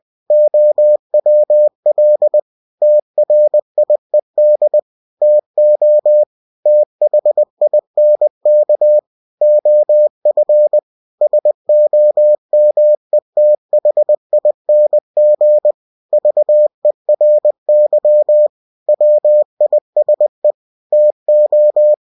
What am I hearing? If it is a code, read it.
Morse code, heard as a keyed tone, about 20 wpm, 600 Hz. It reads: OWL TRIED TO THINK OF SOMETHING VERY WISE TO
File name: cw-books / winnie-the-pooh / winnie-20wpm-page-0102